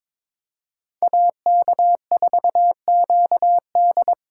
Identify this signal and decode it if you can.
Morse code, heard as a keyed tone, about 21 wpm, 700 Hz. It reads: AK4QD
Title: USA Call Signs - 5 Characters - 3x Character Spacing - 22wpm